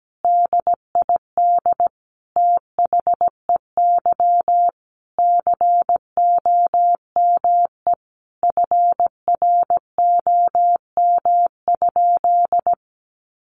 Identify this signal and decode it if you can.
Morse code, heard as a keyed tone, about 17 wpm, 700 Hz. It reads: DID THEY COME FROM?
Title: Sentences from Top 200 Words 17wpm